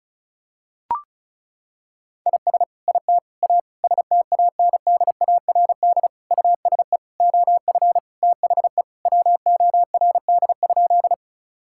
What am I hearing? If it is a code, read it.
Morse code, heard as a keyed tone, about 36 wpm, 700 Hz. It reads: IS IT A STANDARD USE OF THE WORD?
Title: Sentences from Top 1000 Words - Review Part 6 - 35wpm